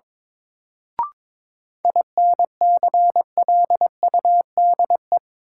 Morse code, heard as a keyed tone, about 22 wpm, 700 Hz. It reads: INCLUDE
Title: Sentences from Top 1000 Words - Review Part 3 - New Word Drill - 22wpm